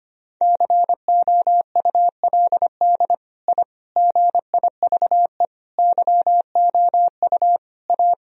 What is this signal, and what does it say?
Morse code, heard as a keyed tone, about 25 wpm, 700 Hz. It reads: COULD I GIVE YOU A
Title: Sentences from Top 1000 Words - Review Part 1 - 25wpm